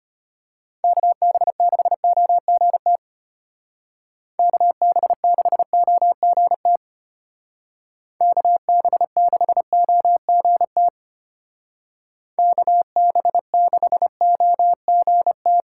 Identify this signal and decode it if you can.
Morse code, heard as a keyed tone, about 29 wpm, 700 Hz. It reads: KB6OGT KB6OGT KB6OGT KB6OGT
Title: Speed Racing - USA 2x3 Call Signs 25wpm